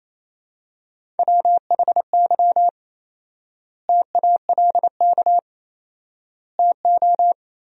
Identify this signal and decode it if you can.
Morse code, heard as a keyed tone, about 28 wpm, 700 Hz. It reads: WHY TALK TO
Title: 4x Word-Spacing - Sentences from Top 400 Words - 28wpm